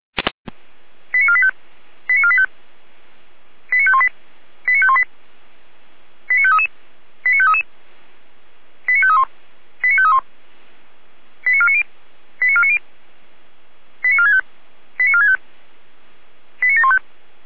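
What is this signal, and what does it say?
Five-tone selective call (ZVEI), heard as quick runs of five tones, tones of about 70 ms. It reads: 89485 89485 89518 89518 89533 89533 89531 89531 89490 89490 89465 89465 89715
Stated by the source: ZVEI five-tone call received by SDR.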